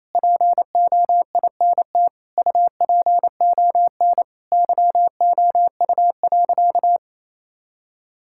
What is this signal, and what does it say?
Morse code, heard as a keyed tone, about 28 wpm, 700 Hz. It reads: POINT UPON YOU.
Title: Sentences from Top 1600 Words 28wpm